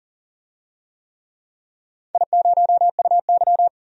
Morse code, heard as a keyed tone, about 42 wpm, 700 Hz. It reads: I0UY
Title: DXCC-100 Call Signs - 4 Characters - 40wpm